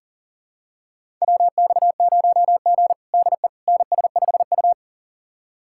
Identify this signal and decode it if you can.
Morse code, heard as a keyed tone, about 42 wpm, 690 Hz. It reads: WX0G DE NS5U
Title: USA Callsign DE Callsign - 4 Characters - 40wpm